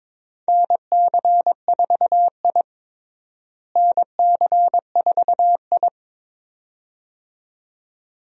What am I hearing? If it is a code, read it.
Morse code, heard as a keyed tone, about 23 wpm, 700 Hz. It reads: NC4I NC4I
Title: USA Call Signs - 4 Characters - Repeated 2x - 22wpm